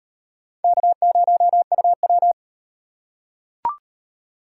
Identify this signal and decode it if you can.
Morse code, heard as a keyed tone, about 33 wpm, 700 Hz. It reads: K0UW